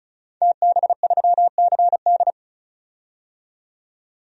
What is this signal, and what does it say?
Morse code, heard as a keyed tone, about 34 wpm, 700 Hz. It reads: TB3CD
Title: DXCC-100 Call Signs - 5 Characters - 35wpm